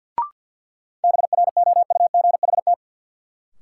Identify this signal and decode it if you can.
Morse code, heard as a keyed tone, about 49 wpm, 700 Hz. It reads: BROUGHT
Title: Sentences from Top 500 Words - New Word Drill - 50wpm